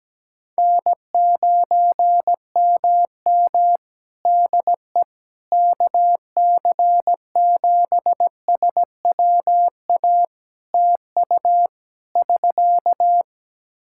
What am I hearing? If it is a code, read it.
Morse code, heard as a keyed tone, about 17 wpm, 700 Hz. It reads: N9MM DE KC7SWA TU <SK>
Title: Practice QSO - Exchange 4 - Ending - 17wpm